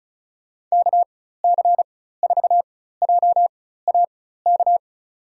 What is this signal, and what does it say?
Morse code, heard as a keyed tone, about 36 wpm, spaced as for 16 wpm, 700 Hz. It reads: KC4JAK
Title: USA Call Signs - 6 Characters - 4x Character Spacing - 35wpm